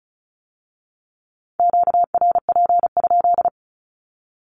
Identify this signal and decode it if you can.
Morse code, heard as a keyed tone, about 36 wpm, 700 Hz. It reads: QRP?
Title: Q-Codes with Spoken Meaning - 35wpm